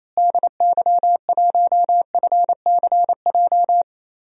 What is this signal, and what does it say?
Morse code, heard as a keyed tone, about 28 wpm, 700 Hz. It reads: DY1FCJ
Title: DXCC-100 Call Signs - 6 Characters - Repeated 3x - 28wpm